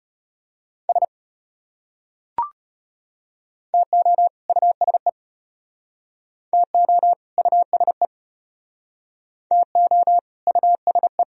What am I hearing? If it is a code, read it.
Morse code, heard as a keyed tone, about 32 wpm, 700 Hz. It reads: S TO USE TO USE TO USE